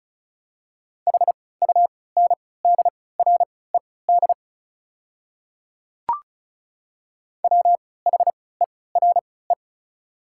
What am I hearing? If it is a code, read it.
Morse code, heard as a keyed tone, about 36 wpm, spaced as for 20 wpm, 700 Hz. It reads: HUNDRED WHERE
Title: 3x Character Spacing - Top 500 Words - 35wpm